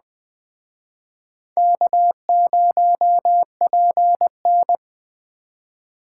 Morse code, heard as a keyed tone, about 21 wpm, 700 Hz. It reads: K0PN
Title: Speed Racing - USA 1x2 Call Signs 17wpm